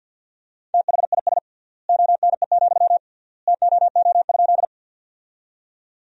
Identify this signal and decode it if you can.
Morse code, heard as a keyed tone, about 49 wpm, 700 Hz. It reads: THIS ONE, TOO?
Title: 3x Word-Spacing - Sentences from Top 200 Words - 50wpm